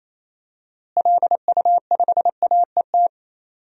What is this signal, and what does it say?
Morse code, heard as a keyed tone, about 28 wpm, 700 Hz. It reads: LU5AET
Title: DXCC-100 Call Signs - 6 Characters - Repeated 2x - 28wpm